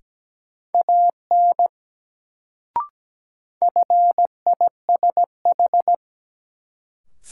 Morse code, heard as a keyed tone, about 17 wpm, 700 Hz. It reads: AN FISH
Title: Sentences from Top 300 Words - New Word Drill - 17wpm